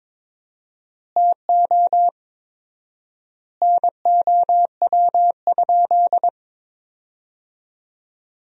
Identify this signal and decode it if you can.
Morse code, heard as a keyed tone, about 22 wpm, 700 Hz. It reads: TO NOW?
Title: 4x Word-Spacing - Sentences from Top 200 Words - 22wpm